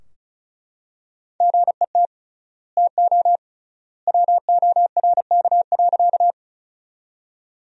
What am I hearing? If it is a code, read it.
Morse code, heard as a keyed tone, about 36 wpm, 700 Hz. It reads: GET TO WORK.
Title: 3x Word-Spacing - Sentences from Top 200 Words - 35wpm